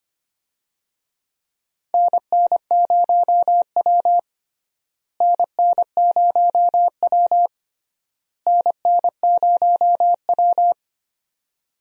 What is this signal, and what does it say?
Morse code, heard as a keyed tone, about 25 wpm, 700 Hz. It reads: NN0W NN0W NN0W
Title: USA Call Signs - 4 Characters - Repeated 3x - 25wpm